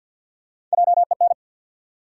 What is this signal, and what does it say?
Morse code, heard as a keyed tone, about 52 wpm, 700 Hz. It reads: JEN
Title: USA Common Names - Max 3 Letters - 50wpm